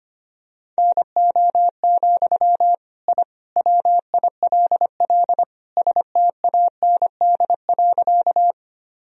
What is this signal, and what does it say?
Morse code, heard as a keyed tone, about 25 wpm, 700 Hz. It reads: NO, I WILL STAND.
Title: Sentences from Top 1000 Words - Review Part 1 - 25wpm